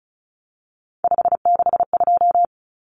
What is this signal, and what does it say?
Morse code, heard as a keyed tone, about 34 wpm, 700 Hz. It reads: 562
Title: Numbers - 3-Digit 35wpm